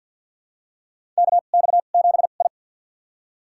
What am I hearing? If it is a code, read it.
Morse code, heard as a keyed tone, about 47 wpm, spaced as for 35 wpm, 700 Hz. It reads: KX7I
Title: USA Call Signs - 4 Characters - 2x Character Spacing - 50wpm